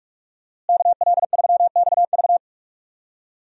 Morse code, heard as a keyed tone, about 44 wpm, 700 Hz. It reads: KL3XV
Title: USA Call Signs - 5 Characters - Repeated 3x - 45wpm